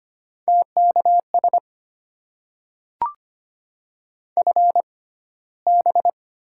Morse code, heard as a keyed tone, about 25 wpm, spaced as for 15 wpm, 700 Hz. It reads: TKS FB
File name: Top 90 QSO Elements - 6x Character Spacing - 25wpm